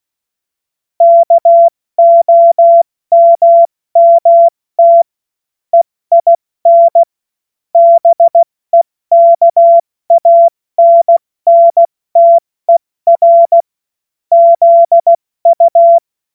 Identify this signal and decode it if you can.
Morse code, heard as a keyed tone, about 16 wpm, 680 Hz. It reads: KOMMT EIN BEKANNTER ZU
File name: Morsekurs / Witze / Witz 29 April 2025 70 F80